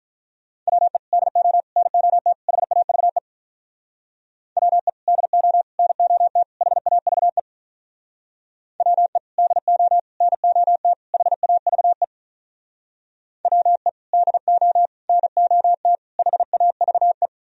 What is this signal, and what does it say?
Morse code, heard as a keyed tone, about 47 wpm, 700 Hz. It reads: WE DO NOT HAVE WE DO NOT HAVE WE DO NOT HAVE WE DO NOT HAVE